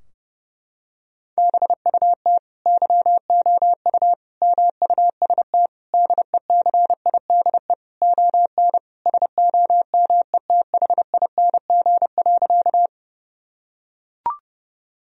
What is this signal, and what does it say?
Morse code, heard as a keyed tone, about 32 wpm, 700 Hz. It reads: BUT YOU MUST DECIDE ON SOMETHING.
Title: Sentences from Top 1000 Words - Review Part 3 - 30wpm